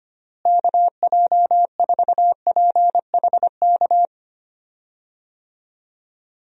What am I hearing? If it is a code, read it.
Morse code, heard as a keyed tone, about 25 wpm, 700 Hz. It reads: KJ4PHK
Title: USA Call Signs - 6 Characters - Repeated 3x - 25wpm